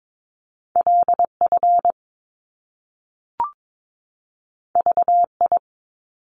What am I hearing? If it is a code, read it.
Morse code, heard as a keyed tone, about 22 wpm, 700 Hz. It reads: LF VI
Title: Top Two Letter Combinations in Words 22wpm